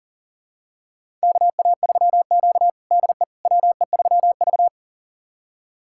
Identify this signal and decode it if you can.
Morse code, heard as a keyed tone, about 37 wpm, 700 Hz. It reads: KA3Q DE WE3V
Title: USA Callsign DE Callsign - 4 Characters - 40wpm